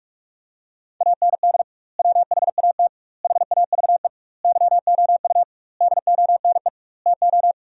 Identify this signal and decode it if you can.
Morse code, heard as a keyed tone, about 46 wpm, 700 Hz. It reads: AND WHAT HAVE YOU DONE TO
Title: 2x Word-Spacing - Sentences from Top 500 Words - 45wpm